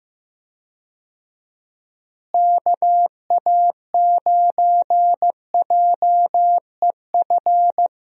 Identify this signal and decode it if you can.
Morse code, heard as a keyed tone, about 15 wpm, 700 Hz. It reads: KA9JEF
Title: USA Call Signs - 6 Characters - Repeated 2x - 15wpm